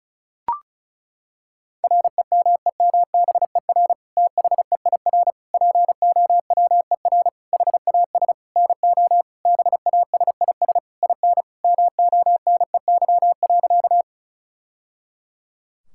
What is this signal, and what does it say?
Morse code, heard as a keyed tone, about 34 wpm, 700 Hz. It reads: REMEMBER THEIR POWER HAS NO BASIS IN MONEY.
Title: Sentences from Top 1000 Words - Review Part 5 - 35wpm